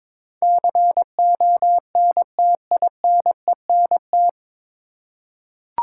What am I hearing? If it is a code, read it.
Morse code, heard as a keyed tone, about 22 wpm, 700 Hz. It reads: CONTINENT
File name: 9 Letter Words - First 3 Letters Given - 22wpm